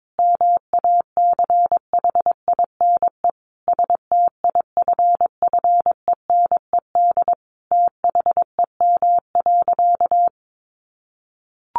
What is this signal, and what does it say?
Morse code, heard as a keyed tone, about 22 wpm, 700 Hz. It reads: MACHINE STIFFENED THEM.
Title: A Sound Of Thunder 22wpm